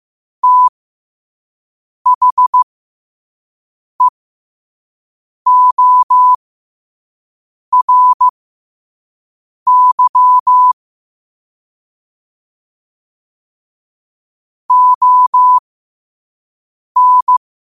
Morse code, heard as a keyed tone, about 15 wpm, spaced as for 5 wpm, 1000 Hz. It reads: THEORY ON
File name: qotdfiles / 5wpm